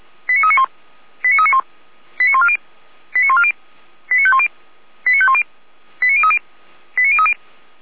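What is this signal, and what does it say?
Five-tone selective call (ZVEI), heard as quick runs of five tones, tones of about 80 ms. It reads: 89391 89391 89140 89140 89510 89510 89030 89030
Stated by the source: ZVEI five-tone selective call received by SDR.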